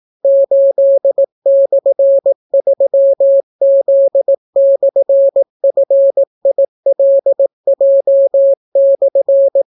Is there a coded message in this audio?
Yes: Morse code from 0.2 to 9.6 s, heard as a keyed tone, about 18 wpm, 550 Hz. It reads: 8/3Z/FILJ/